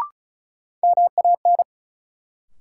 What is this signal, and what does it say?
Morse code, heard as a keyed tone, about 36 wpm, 700 Hz. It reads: MAN
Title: Sentences from Top 200 Words - New Word Drill - 35wpm